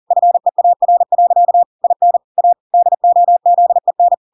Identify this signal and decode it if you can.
Morse code, heard as a keyed tone, about 40 wpm, 700 Hz. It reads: FEAR. IN A DOZEN